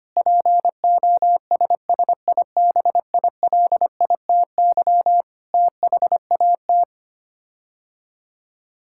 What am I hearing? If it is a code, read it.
Morse code, heard as a keyed tone, about 25 wpm, 700 Hz. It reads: POSSIBILITY THAT
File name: Sets of 9 Words - 25wpm